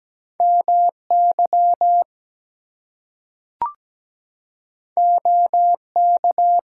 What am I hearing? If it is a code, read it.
Morse code, heard as a keyed tone, about 17 wpm, 700 Hz. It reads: MY OK
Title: Two Letter Words - 17wpm